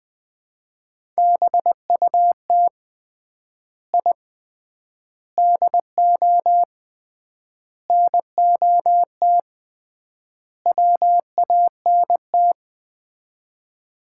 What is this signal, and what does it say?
Morse code, heard as a keyed tone, about 20 wpm, 700 Hz. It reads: BUT I DO NOT WANT
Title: 3x Word-Spacing - Sets of 5 Words - 20wpm